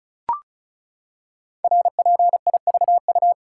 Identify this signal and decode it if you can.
Morse code, heard as a keyed tone, about 36 wpm, 690 Hz. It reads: RPIVU
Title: Code Groups - 5 Letters - 35wpm